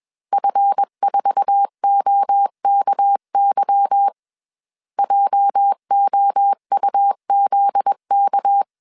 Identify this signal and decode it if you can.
Morse code, heard as a keyed tone, about 21 wpm, 800 Hz. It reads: F4OKY JOUZK
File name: CW / DET / 1 9 1500C S C 20wpm 2310